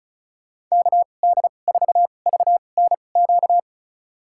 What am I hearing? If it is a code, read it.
Morse code, heard as a keyed tone, about 36 wpm, spaced as for 25 wpm, 700 Hz. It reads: KD4VNQ
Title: USA Call Signs - 2x Character Spacing - 35wpm